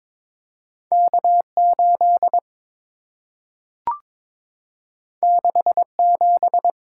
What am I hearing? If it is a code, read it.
Morse code, heard as a keyed tone, about 22 wpm, 700 Hz. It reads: K8 67
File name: practice / 22wpm / Two Letter-Number Combinations - 22wpm